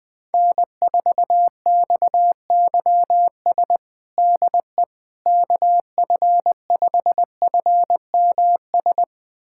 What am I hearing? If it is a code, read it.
Morse code, heard as a keyed tone, about 20 wpm, 700 Hz. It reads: N4XYS DE KF5FMS
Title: USA Callsign DE Callsign 20wpm